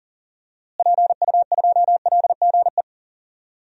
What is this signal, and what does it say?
Morse code, heard as a keyed tone, about 40 wpm, 700 Hz. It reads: PU2LGE
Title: DXCC-100 Call Signs - 6 Characters - Repeated 2x - 40wpm